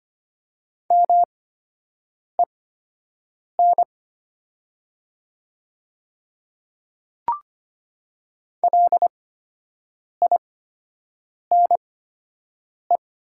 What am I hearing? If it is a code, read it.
Morse code, heard as a keyed tone, about 25 wpm, spaced as for 7 wpm, 690 Hz. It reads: MEN LINE
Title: 8x Character Spacing - Top 200 Words - 25wpm